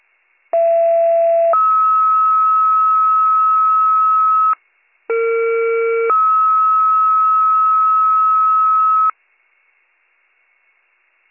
Two-tone page, heard as a one-second tone, then a three-second tone, about 670 then 1290 Hz, about 460 then 1290 Hz. It is heard from 0.5 to 9.1 s.